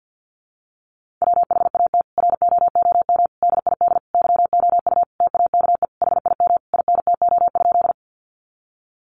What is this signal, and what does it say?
Morse code, heard as a keyed tone, about 50 wpm, 700 Hz. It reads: WHAT ROOM DID YOU TAKE HIM INTO?
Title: Sentences from Top 600 Words 50wpm